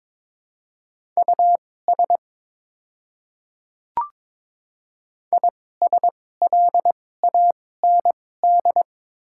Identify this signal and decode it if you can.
Morse code, heard as a keyed tone, about 22 wpm, spaced as for 16 wpm, 700 Hz. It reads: US ISLAND
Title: 2x Character Spacing - Top 500 Words - 22wpm